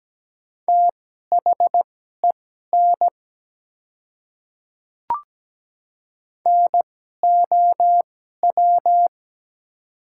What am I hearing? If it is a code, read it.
Morse code, heard as a keyed tone, about 17 wpm, spaced as for 12 wpm, 700 Hz. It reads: THEN NOW
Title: 2x Character Spacing - Top 100 Words - 17wpm